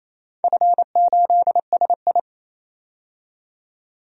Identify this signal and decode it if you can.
Morse code, heard as a keyed tone, about 28 wpm, 700 Hz. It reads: F8SI
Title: DXCC-100 Call Signs - 4 Characters - 28wpm